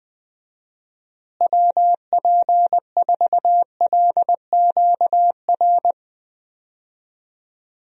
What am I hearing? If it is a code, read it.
Morse code, heard as a keyed tone, about 20 wpm, 700 Hz. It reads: WP4LQR
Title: USA Call Signs - 6 Characters - Repeated 3x - 20wpm